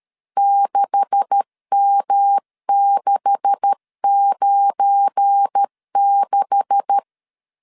Morse code, heard as a keyed tone, about 13 wpm, 800 Hz. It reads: 6M696
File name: LEZIONI / 9 286 12wpm 2305